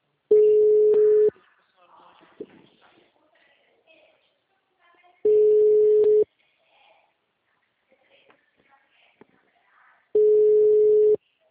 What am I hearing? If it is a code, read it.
Ringback tone (Europe, 425 Hz), with speech.
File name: out- -2015- - - .77162